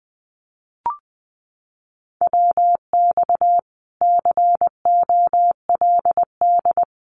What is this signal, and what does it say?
Morse code, heard as a keyed tone, about 20 wpm, 700 Hz. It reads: WX COLD
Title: Common QSO Phrases 20wpm